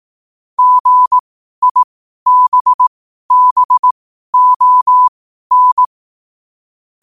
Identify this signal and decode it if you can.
Morse code, heard as a keyed tone, about 18 wpm, spaced as for 12 wpm, 1000 Hz. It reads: GIBBON
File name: qotdfiles / 12wpm